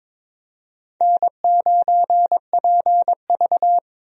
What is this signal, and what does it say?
Morse code, heard as a keyed tone, about 22 wpm, 700 Hz. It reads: N9PV
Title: USA Call Signs - 4 Characters - 22wpm